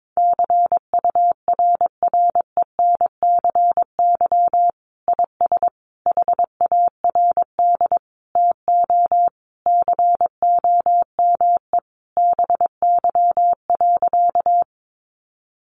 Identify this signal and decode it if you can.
Morse code, heard as a keyed tone, about 22 wpm, 700 Hz. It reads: CURRENCY IS HARD TO COME BY.
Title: Sentences from Top 1500 Words 22wpm